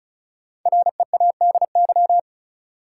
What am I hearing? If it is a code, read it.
Morse code, heard as a keyed tone, about 36 wpm, 700 Hz. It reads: READY
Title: Sentences from Top 400 Words - New Word Drill - 35wpm